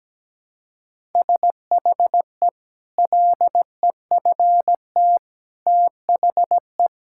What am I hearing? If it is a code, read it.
Morse code, heard as a keyed tone, about 17 wpm, 700 Hz. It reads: SHE LEFT THE
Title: Sentences from Top 1000 Words - Review Part 5 - 17wpm